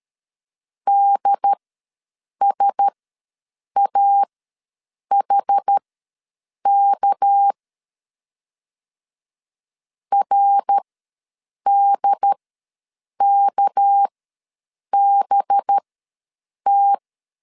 Morse code, heard as a keyed tone, about 13 wpm, spaced as for 7 wpm, 800 Hz. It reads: DSAHK RDKBT
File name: LEZIONI / 6 PKX0 / 6 PK ws 12wpm 3116